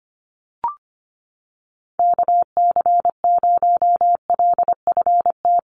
Morse code, heard as a keyed tone, about 25 wpm, 700 Hz. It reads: KC0LFT
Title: USA Call Signs Repeated 2x 25wpm